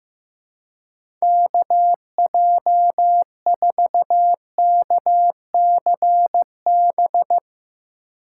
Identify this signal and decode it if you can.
Morse code, heard as a keyed tone, about 15 wpm, 700 Hz. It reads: KJ4KCB
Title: USA Call Signs - 6 Characters - 3x Character Spacing - 15wpm